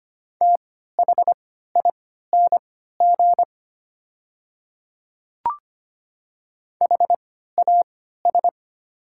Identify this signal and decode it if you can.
Morse code, heard as a keyed tone, about 25 wpm, spaced as for 14 wpm, 700 Hz. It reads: THING HAS
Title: 3x Character Spacing - Top 100 Words - 25wpm